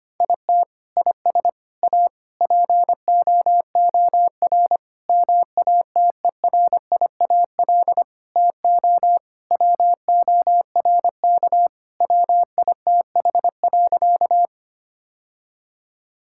Morse code, heard as a keyed tone, about 25 wpm, 700 Hz. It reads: IT IS A POOR MATERIAL TO WORK WITH.
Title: Sentences from Top 1000 Words - Review Part 5 - 25wpm